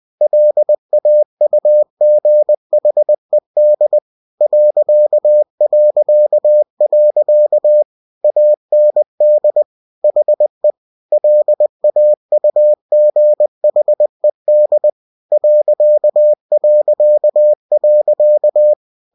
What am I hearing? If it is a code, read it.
Morse code, heard as a keyed tone, about 20 wpm, 600 Hz. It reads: LAUGHED ... AND HE LAUGHED ...
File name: cw-books / winnie-the-pooh / winnie-20wpm-page-0043